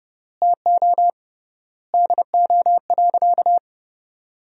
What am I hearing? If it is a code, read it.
Morse code, heard as a keyed tone, about 30 wpm, 700 Hz. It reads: TO DO.
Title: 3x Word-Spacing - Sentences from Top 100 Words - 30wpm